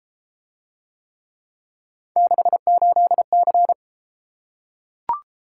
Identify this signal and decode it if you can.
Morse code, heard as a keyed tone, about 32 wpm, 700 Hz. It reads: 68C